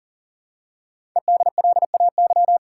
Morse code, heard as a keyed tone, about 38 wpm, 700 Hz. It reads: EDLAY
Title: Code Groups - 5 Letters - 40wpm